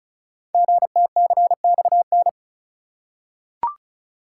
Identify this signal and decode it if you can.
Morse code, heard as a keyed tone, about 36 wpm, 690 Hz. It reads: GTCXN
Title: Code Groups - 5 Letters - 35wpm